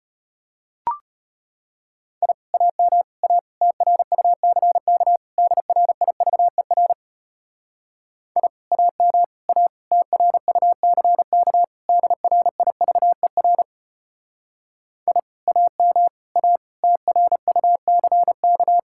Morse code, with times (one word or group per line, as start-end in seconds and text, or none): 2.2-2.3 I
2.5-3.0 AM
3.2-3.4 A
3.6-5.2 TRUCK
5.4-6.9 DRIVER
8.4-8.5 I
8.7-9.2 AM
9.5-9.7 A
9.9-11.6 TRUCK
11.9-13.6 DRIVER
15.1-15.2 I
15.5-16.1 AM
16.4-16.6 A
16.8-18.8 TRUCK